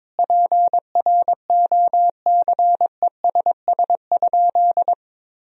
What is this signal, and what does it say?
Morse code, heard as a keyed tone, about 22 wpm, 700 Hz. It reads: PROCESS?
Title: Sentences from Top 1000 Words - Review Part 3 - 22wpm